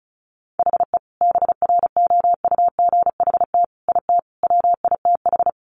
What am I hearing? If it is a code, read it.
Morse code, heard as a keyed tone, about 34 wpm, 700 Hz. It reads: HE BROUGHT IT WITH
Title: Sentences from Top 500 Words 35wpm